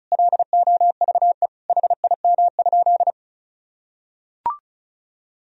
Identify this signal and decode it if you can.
Morse code, heard as a keyed tone, about 36 wpm, 700 Hz. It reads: LOVE HIM?
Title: Sentences from Top 400 Words and Abbreviations 35wpm